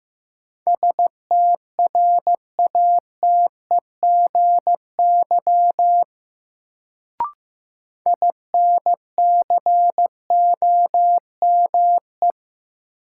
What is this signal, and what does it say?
Morse code, heard as a keyed tone, about 15 wpm, 700 Hz. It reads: STRATEGY INCOME
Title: Sentences from Top 1000 Words - Review Part 5 - New Word Drill - 15wpm